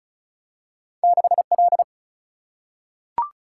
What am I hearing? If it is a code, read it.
Morse code, heard as a keyed tone, about 36 wpm, 700 Hz. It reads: 6L
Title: Single-Letter Single-Number 35wpm